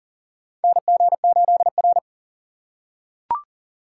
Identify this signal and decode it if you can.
Morse code, heard as a keyed tone, about 40 wpm, 690 Hz. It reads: NG8R
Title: USA Call Signs - 4x Character Spacing - 40wpm